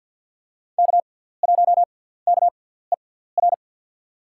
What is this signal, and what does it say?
Morse code, heard as a keyed tone, about 50 wpm, spaced as for 17 wpm, 700 Hz. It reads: K1KER
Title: USA Call Signs - 5 Characters - 6x Character Spacing - 50wpm